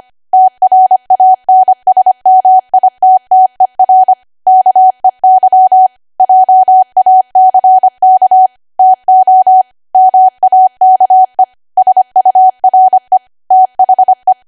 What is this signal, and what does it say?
Morse code, heard as a keyed tone, about 25 wpm, 750 Hz. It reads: TRANSMITTER KEY JACK TO MAKE SURE THE
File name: Archive / 25 WPM / 25WPM